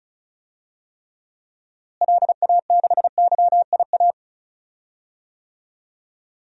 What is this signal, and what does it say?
Morse code, heard as a keyed tone, about 35 wpm, 700 Hz. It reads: LA6YIA
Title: DXCC-100 Call Signs - 6 Characters - Repeated 3x - 35wpm